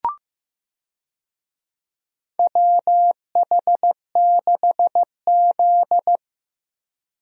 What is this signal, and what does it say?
Morse code, heard as a keyed tone, about 15 wpm, 700 Hz. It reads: WH6Z
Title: USA Call Signs - 4 Characters - Repeated 2x - 15wpm